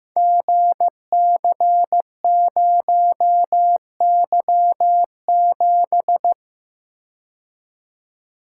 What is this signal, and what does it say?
Morse code, heard as a keyed tone, about 15 wpm, 700 Hz. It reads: GC0Y7
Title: Code Groups - 5 Letter:Numbers - 15wpm